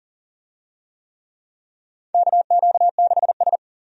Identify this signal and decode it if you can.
Morse code, heard as a keyed tone, about 40 wpm, 700 Hz. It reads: KQ6S
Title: USA Call Signs - 4 Characters - 40wpm